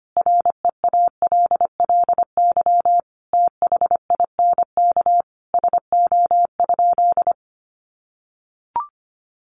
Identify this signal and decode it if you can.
Morse code, heard as a keyed tone, about 25 wpm, 700 Hz. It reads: REALLY THINK SO?
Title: Sentences from Top 600 Words 25wpm